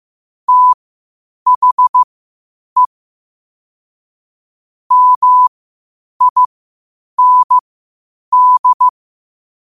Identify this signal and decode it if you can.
Morse code, heard as a keyed tone, about 15 wpm, spaced as for 8 wpm, 1000 Hz. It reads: THE MIND
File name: qotdfiles / 8wpm